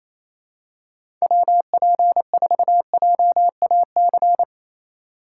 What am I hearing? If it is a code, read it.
Morse code, heard as a keyed tone, about 28 wpm, 700 Hz. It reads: WP4JAC
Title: USA Call Signs - 6 Characters - 8x Character Spacing - 28wpm